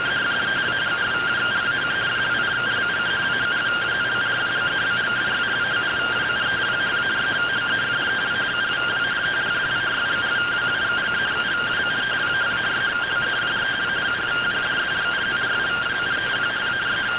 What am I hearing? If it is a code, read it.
Two-tone teleprinter signal, about 170 Hz shift.